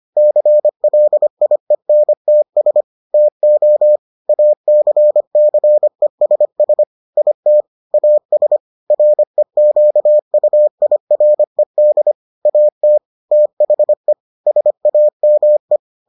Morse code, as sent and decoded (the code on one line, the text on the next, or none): -.-. .-.. .. -. - ... / - --- / .- -.-. -.-. ... ... / .. - / .- ... / .-. --.- ..- .. .-. -.. / .- - / - .... / ... .- -- .
CLIENTS TO ACCESS IT AS REQUIRED AT THE SAME